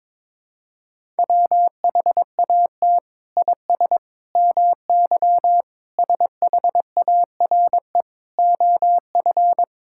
Morse code, heard as a keyed tone, about 22 wpm, 700 Hz. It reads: WHAT IS MY SHARE OF
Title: Sentences from Top 1000 Words - Review Part 6 - 22wpm